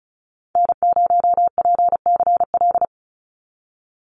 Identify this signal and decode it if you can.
Morse code, heard as a keyed tone, about 34 wpm, 700 Hz. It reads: N0PCL
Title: NA SOTA Activator Callsigns 35wpm